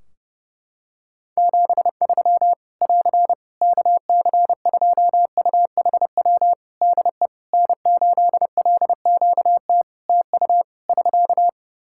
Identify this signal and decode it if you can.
Morse code, heard as a keyed tone, about 30 wpm, 700 Hz. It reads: 73 <AR> KC2UHW DE N8LQT TU <SK>
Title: Practice QSO - Exchange 4 - Ending - 30wpm